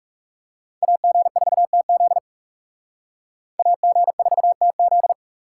Morse code, heard as a keyed tone, about 37 wpm, 700 Hz. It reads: AG4TZ AG4TZ
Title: Speed Racing - USA 2x2 Call Signs 30wpm